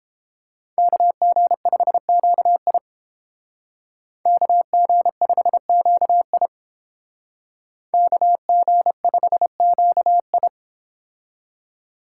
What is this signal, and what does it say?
Morse code, heard as a keyed tone, about 28 wpm, 700 Hz. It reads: KG5QI KG5QI KG5QI